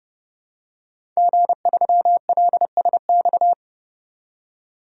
Morse code, heard as a keyed tone, about 30 wpm, 700 Hz. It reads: G3LSX
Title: DXCC-100 Call Signs - 5 Characters - 30wpm